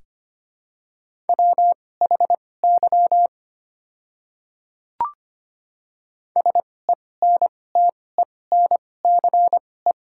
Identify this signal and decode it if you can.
Morse code, heard as a keyed tone, about 25 wpm, spaced as for 18 wpm, 700 Hz. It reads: WHY SENTENCE
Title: 2x Character Spacing - Top 200 Words - 25wpm